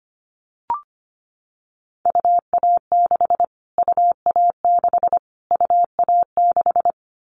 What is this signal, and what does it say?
Morse code, heard as a keyed tone, about 25 wpm, 700 Hz. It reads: UA6 UA6 UA6
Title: Callsign Prefixes for Top 25 DXCC - Repeated 3x - 25wpm